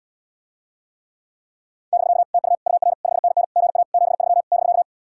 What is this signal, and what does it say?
Morse code, heard as a keyed tone, about 77 wpm, 700 Hz. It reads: <BK> TU UR 5NN GA 73 <BK>